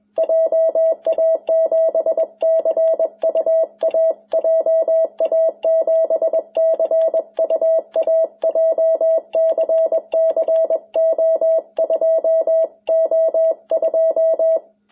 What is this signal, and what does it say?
Morse code, heard as a keyed tone, about 20 wpm, 650 Hz. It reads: JA7CUA JA7CUA JCC O2O2
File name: jh8cbh / 2021cw